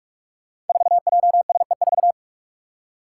Morse code, heard as a keyed tone, about 46 wpm, 700 Hz. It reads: 4JSE4
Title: Five Challenging Letter-Number Combinations 45wpm